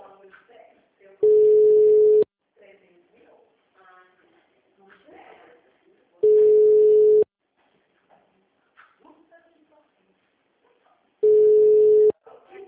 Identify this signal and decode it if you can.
Ringback tone (Europe, 425 Hz), with speech.
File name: out- -2003- - - .5267